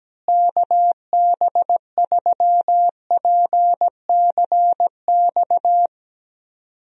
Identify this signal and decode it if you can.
Morse code, heard as a keyed tone, about 17 wpm, 700 Hz. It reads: KB3PCX
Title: USA Call Signs - 6 Characters - Repeated 3x - 17wpm